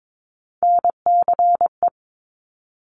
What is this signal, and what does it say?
Morse code, heard as a keyed tone, about 22 wpm, 700 Hz. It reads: NCE
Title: Top Three Letter Combinations in Words 22wpm